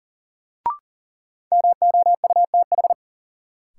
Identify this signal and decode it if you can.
Morse code, heard as a keyed tone, about 40 wpm, 700 Hz. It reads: MOUTH
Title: Sentences from Top 1500 Words - New Word Drill Alt - 40wpm